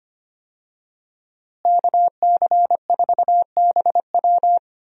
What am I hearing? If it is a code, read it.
Morse code, heard as a keyed tone, about 25 wpm, 700 Hz. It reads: KC4BW
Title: USA Call Signs - 5 Characters - Repeated 3x - 25wpm